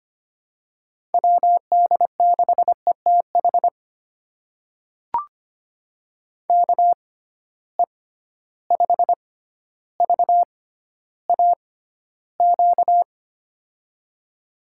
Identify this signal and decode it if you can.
Morse code, heard as a keyed tone, about 25 wpm, spaced as for 13 wpm, 700 Hz. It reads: WD6ETH KE5VAQ
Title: USA Call Signs - 6 Characters - 6x Character Spacing - 25wpm